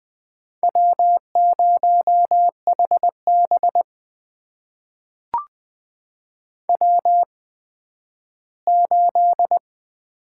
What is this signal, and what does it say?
Morse code, heard as a keyed tone, about 20 wpm, spaced as for 12 wpm, 700 Hz. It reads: W0HB W8
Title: USA Call Signs - 4 Characters - 8x Character Spacing - 20wpm